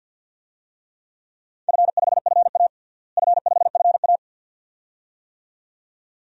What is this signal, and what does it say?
Morse code, heard as a keyed tone, about 53 wpm, 700 Hz. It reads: F5FA F5FA
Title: DXCC-100 Call Signs - 4 Characters - Repeated 2x - 50wpm